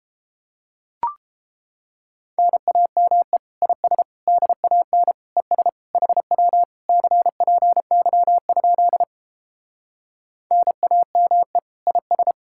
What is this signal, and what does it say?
Morse code, heard as a keyed tone, about 29 wpm, 700 Hz. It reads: NAME IS DAN ES HW CPY? NAME IS